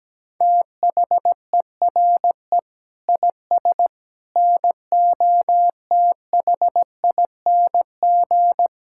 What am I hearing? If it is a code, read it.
Morse code, heard as a keyed tone, about 17 wpm, 700 Hz. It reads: THERE IS NOTHING
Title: Sentences from Top 1000 Words - Review Part 6 - 17wpm